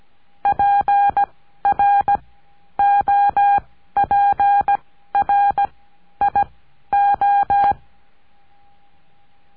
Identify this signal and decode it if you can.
Morse code, heard as a keyed tone, about 17 wpm, spaced as for 11 wpm, 810 Hz. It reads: PROPRIO
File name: cursos / CURSO DE TELEGRAFIA / Telegrafia - Trilha 17